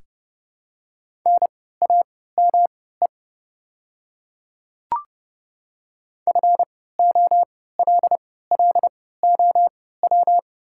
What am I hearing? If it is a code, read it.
Morse code, heard as a keyed tone, about 30 wpm, spaced as for 17 wpm, 690 Hz. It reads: NAME FOLLOW
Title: 3x Character Spacing - Top 200 Words - 30wpm